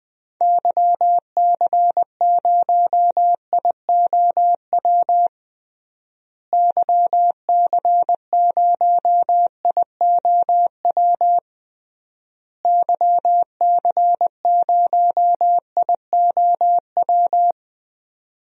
Morse code, heard as a keyed tone, about 20 wpm, 700 Hz. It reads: YC0IOW YC0IOW YC0IOW
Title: DXCC-100 Call Signs - 6 Characters - Repeated 3x - 20wpm